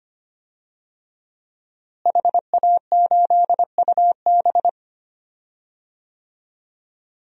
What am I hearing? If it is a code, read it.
Morse code, heard as a keyed tone, about 25 wpm, 700 Hz. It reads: HA8UB
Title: DXCC-100 Call Signs - 5 Characters - Repeated 3x - 25wpm